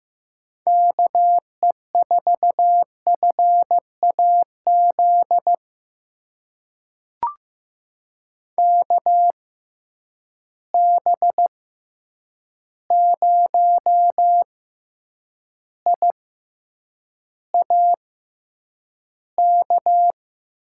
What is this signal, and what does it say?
Morse code, heard as a keyed tone, about 15 wpm, spaced as for 8 wpm, 700 Hz. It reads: KE4FAZ KB0IAK
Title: USA Call Signs - 6 Characters - 6x Character Spacing - 15wpm